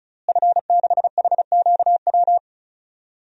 Morse code, heard as a keyed tone, about 34 wpm, 700 Hz. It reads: F6HQW
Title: DXCC-100 Call Signs - 5 Characters - Repeated 2x - 35wpm